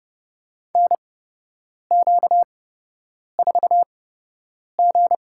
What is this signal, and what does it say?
Morse code, heard as a keyed tone, about 30 wpm, spaced as for 8 wpm, 700 Hz. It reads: NQ4G
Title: USA Call Signs - 4 Characters - 8x Character Spacing - 30wpm